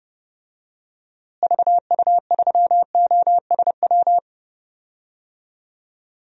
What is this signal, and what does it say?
Morse code, heard as a keyed tone, about 30 wpm, 700 Hz. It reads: VU3OSW
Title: DXCC-100 Call Signs - 6 Characters - Repeated 2x - 30wpm